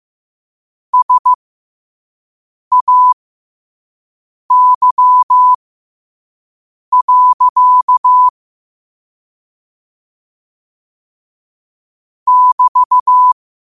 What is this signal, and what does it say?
Morse code, heard as a keyed tone, about 15 wpm, spaced as for 5 wpm, 1000 Hz. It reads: SAY. <BT>
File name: qotdfiles / 5wpm